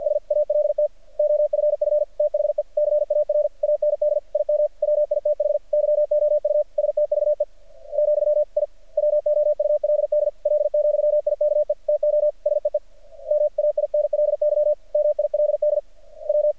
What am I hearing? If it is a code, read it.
Morse code, heard as a keyed tone, about 50 wpm, 600 Hz. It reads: S ALT OFF THE CAR AND IM WITH YOU STVE , I WOULD L,IKE TO SEE MAINLY MILD W
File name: archived / CRSnet